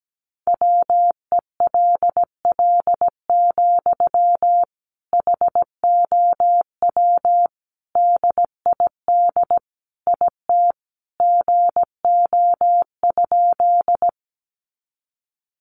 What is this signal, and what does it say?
Morse code, heard as a keyed tone, about 17 wpm, 700 Hz. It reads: WELL, HOW DID IT GO?
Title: Sentences from Top 200 Words 17wpm